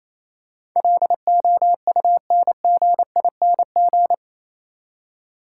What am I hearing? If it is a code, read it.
Morse code, heard as a keyed tone, about 27 wpm, 690 Hz. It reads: LOUNGING
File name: Easy Words - Speed-Racing with Hints - 22wpm